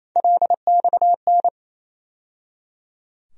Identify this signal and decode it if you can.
Morse code, heard as a keyed tone, about 28 wpm, 700 Hz. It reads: LXN
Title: USA Call Sign Postfixes - 3 Letters 28wpm